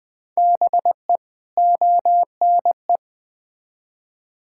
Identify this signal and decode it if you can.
Morse code, heard as a keyed tone, about 20 wpm, 700 Hz. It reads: BE ONE
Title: Build A Phrase - 4-Word Phrases - 20wpm